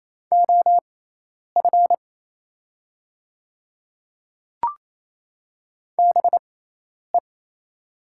Morse code, heard as a keyed tone, about 28 wpm, spaced as for 10 wpm, 690 Hz. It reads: OF BE
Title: 6x Character Spacing - Top 100 Words - 28wpm